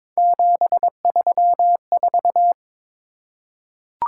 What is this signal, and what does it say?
Morse code, heard as a keyed tone, about 22 wpm, 700 Hz. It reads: 734
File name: practice / 22wpm / Three Letter-Number Combinations - 22wpm